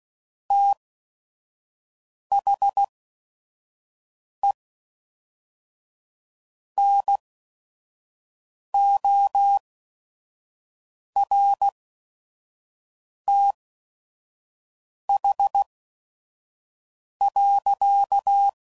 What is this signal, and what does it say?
Morse code, heard as a keyed tone, about 16 wpm, spaced as for 5 wpm, 800 Hz. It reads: THE NORTH.
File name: CW Code Groups @5WPM / Weather Sets / Weather 4 24